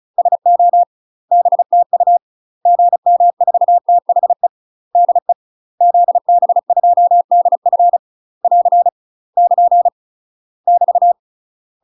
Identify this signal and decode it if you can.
Morse code, heard as a keyed tone, about 36 wpm, 700 Hz. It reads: SO BTU GM4THE DE ZB2DF <AR> <KN> <BT>
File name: UK MASTER QSOs TRAINING / 35WPM / UK MASTER QSOs 0023